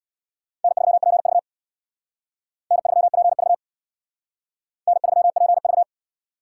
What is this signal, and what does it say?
Morse code, heard as a keyed tone, about 65 wpm, 700 Hz. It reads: N3PV N3PV N3PV